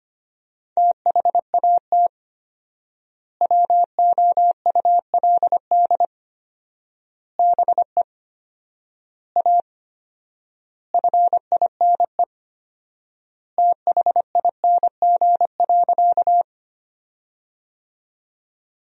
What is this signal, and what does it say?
Morse code, heard as a keyed tone, about 25 wpm, 700 Hz. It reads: THAT WOULD BE A FINE THING.
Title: 4x Word-Spacing - Sentences from Top 500 Words - 25wpm